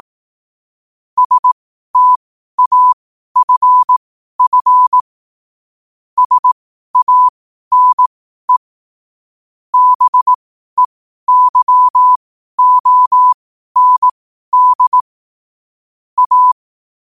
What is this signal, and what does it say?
Morse code, heard as a keyed tone, about 18 wpm, spaced as for 12 wpm, 1000 Hz. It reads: STAFF SANE BEYOND A
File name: qotdfiles / 12wpm